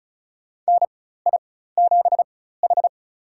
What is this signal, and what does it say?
Morse code, heard as a keyed tone, about 36 wpm, spaced as for 16 wpm, 700 Hz. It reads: NI7H
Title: USA Call Signs - 4 Characters - 4x Character Spacing - 35wpm